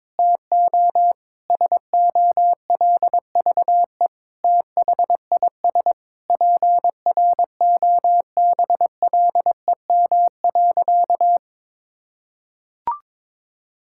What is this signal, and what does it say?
Morse code, heard as a keyed tone, about 21 wpm, 700 Hz. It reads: TO SOLVE THIS PROBLEM.
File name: Sentences with 7-Word Phrases - 22wpm